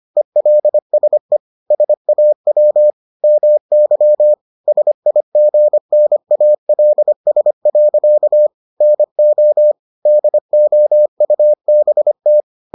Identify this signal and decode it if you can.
Morse code, heard as a keyed tone, about 25 wpm, 600 Hz. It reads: ELSE SAW MY SIGNALS. NO DOUBT